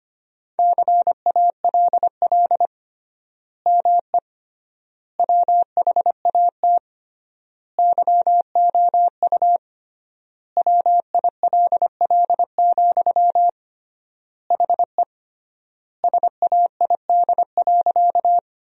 Morse code, heard as a keyed tone, about 25 wpm, 700 Hz. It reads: CALL ME WHAT YOU WILL, HE SAID.
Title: 3x Word-Spacing - Sentences from Top 200 Words - 25wpm